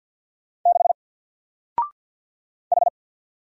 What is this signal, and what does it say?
Morse code, heard as a keyed tone, about 50 wpm, 700 Hz. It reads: 6 H
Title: Single Letter-Number Alt 50wpm